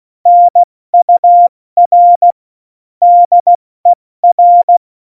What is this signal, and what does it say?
Morse code, heard as a keyed tone, about 16 wpm, spaced as for 14 wpm, 700 Hz. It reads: NUR DER